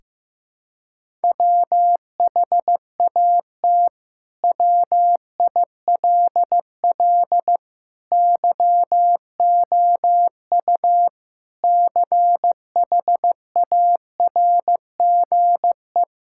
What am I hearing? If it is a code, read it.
Morse code, heard as a keyed tone, about 15 wpm, 700 Hz. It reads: WHAT WILL YOU CHARGE
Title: Sentences from Top 1000 Words - Review Part 6 - 15wpm